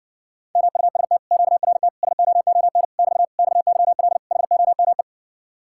Morse code, heard as a keyed tone, about 60 wpm, 700 Hz. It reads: MUST QRT SOON <BT> XYL HOME